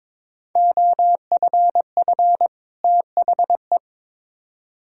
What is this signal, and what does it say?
Morse code, heard as a keyed tone, about 21 wpm, 700 Hz. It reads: OFF THE
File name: Build A Phrase - 2-Word Phrases - 22wpm